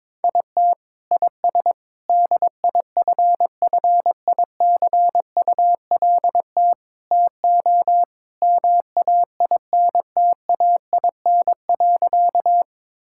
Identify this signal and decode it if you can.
Morse code, heard as a keyed tone, about 22 wpm, 700 Hz. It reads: IT IS DIFFICULT TO MAINTAIN.
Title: Sentences from Top 1000 Words - Review Part 5 - 22wpm